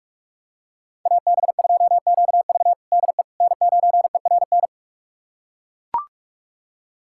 Morse code, heard as a keyed tone, about 42 wpm, 700 Hz. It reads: AB2QV DE N9ERN
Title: USA Callsign DE Callsign - 5 Characters - 45wpm